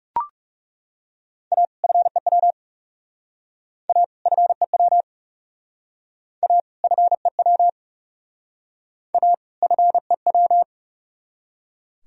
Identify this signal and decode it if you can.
Morse code, heard as a keyed tone, about 38 wpm, 700 Hz. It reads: A FEW A FEW A FEW A FEW